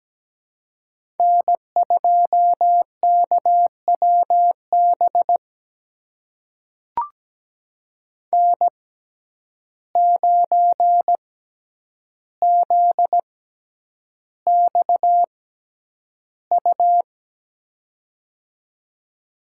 Morse code, heard as a keyed tone, about 17 wpm, spaced as for 9 wpm, 700 Hz. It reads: N2KWB N9ZXU
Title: USA Call Signs - 5 Characters - 6x Character Spacing - 17wpm